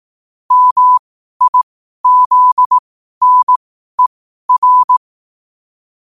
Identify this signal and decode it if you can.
Morse code, heard as a keyed tone, about 18 wpm, spaced as for 12 wpm, 1000 Hz. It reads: MIZNER
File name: qotdfiles / 12wpm